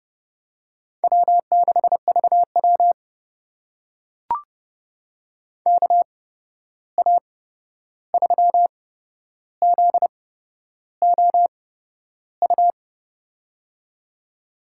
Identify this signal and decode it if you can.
Morse code, heard as a keyed tone, about 30 wpm, spaced as for 11 wpm, 700 Hz. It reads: W6VW KA3ZOU
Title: USA Call Signs - 8x Character Spacing - 30wpm